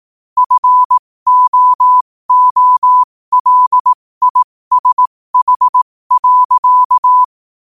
Morse code, heard as a keyed tone, about 18 wpm, spaced as for 15 wpm, 1000 Hz. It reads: FOOLISH.
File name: qotdfiles / 15wpm